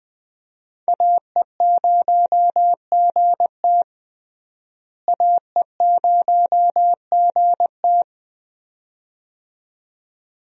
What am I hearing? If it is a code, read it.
Morse code, heard as a keyed tone, about 20 wpm, 700 Hz. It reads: AE0GT AE0GT
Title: USA Call Signs - 5 Characters - Repeated 2x - 20wpm